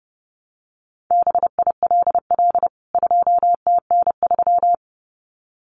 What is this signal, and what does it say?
Morse code, heard as a keyed tone, about 30 wpm, 700 Hz. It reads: BILL 2TN3
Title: CWT Exchange 30wpm